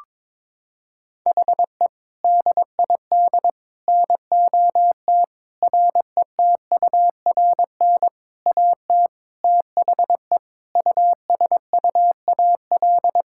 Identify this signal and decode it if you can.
Morse code, heard as a keyed tone, about 22 wpm, 700 Hz. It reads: HE DID NOT RETURN AT THE USUAL
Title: Sentences from Top 1600 Words 22wpm